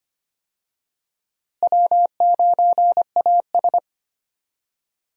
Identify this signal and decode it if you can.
Morse code, heard as a keyed tone, about 25 wpm, 700 Hz. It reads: W9AS
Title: USA Call Signs - 4 Characters - Repeated 3x - 25wpm